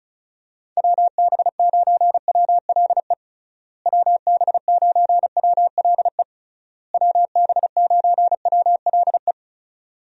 Morse code, heard as a keyed tone, about 34 wpm, 700 Hz. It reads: WB9WLE WB9WLE WB9WLE
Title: USA Call Signs - 6 Characters - Repeated 3x - 35wpm